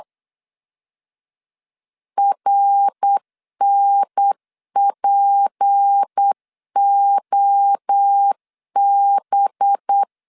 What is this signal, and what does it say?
Morse code, heard as a keyed tone, about 8 wpm, 800 Hz. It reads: RNPOB
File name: LEZIONI / 7 GF9Z / 7 GF9 8wpm 3545